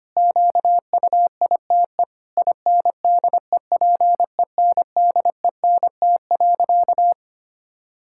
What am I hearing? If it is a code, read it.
Morse code, heard as a keyed tone, about 25 wpm, 700 Hz. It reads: QUITE INDEPENDENT.
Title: Sentences from Top 1000 Words - Review Part 6 - 25wpm